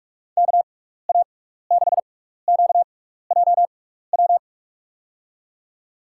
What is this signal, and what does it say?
Morse code, heard as a keyed tone, about 44 wpm, spaced as for 15 wpm, 700 Hz. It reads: KA6QJW
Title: USA Call Signs - 6 Characters - 6x Character Spacing - 45wpm